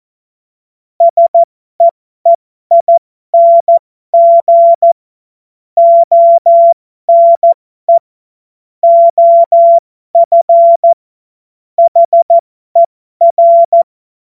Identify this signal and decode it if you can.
Morse code, heard as a keyed tone, about 14 wpm, spaced as for 12 wpm, 680 Hz. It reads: SEEING ONE OF HER
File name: Morsekurs / Witze